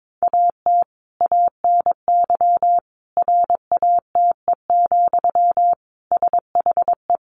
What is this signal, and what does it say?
Morse code, heard as a keyed tone, about 22 wpm, 700 Hz. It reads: AT ANY RATE, SHE
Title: Sentences from Top 1300 Words 22wpm